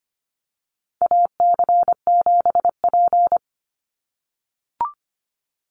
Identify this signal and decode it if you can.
Morse code, heard as a keyed tone, about 25 wpm, 700 Hz. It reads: AC7P
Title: NA SOTA Chaser Callsigns 25wpm